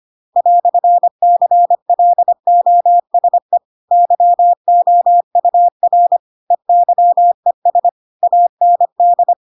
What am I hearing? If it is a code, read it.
Morse code, heard as a keyed tone, about 25 wpm, 700 Hz. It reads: "CLOSE YOUR EYES AND